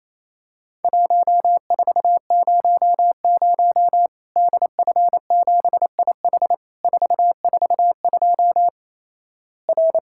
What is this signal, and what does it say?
Morse code, heard as a keyed tone, about 28 wpm, 700 Hz. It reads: 1400 DF7IH 442 R
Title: Worked All Europe DX Contest - QTC Exchange - 28wpm